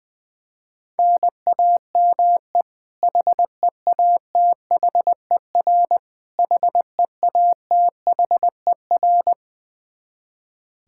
Morse code, heard as a keyed tone, about 20 wpm, 700 Hz. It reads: NAME HEATHER HEATHER
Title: K1USN SST QSO Practice 20wpm